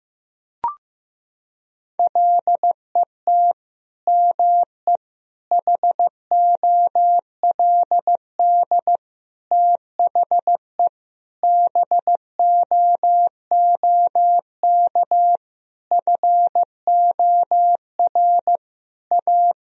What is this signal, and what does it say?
Morse code, heard as a keyed tone, about 15 wpm, 700 Hz. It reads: LET ME HOLD THE BOOK FOR A
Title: Sentences from Top 1000 Words - Review Part 2 - 15wpm